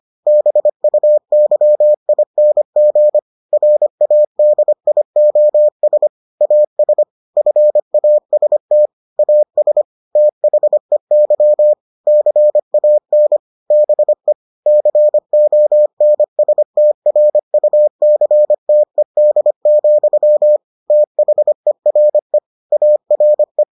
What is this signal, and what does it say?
Morse code, heard as a keyed tone, about 25 wpm, 600 Hz. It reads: BUYING RADIOS AS FAST AS THEY CAN BE CONSTRUCTED, THERE ARE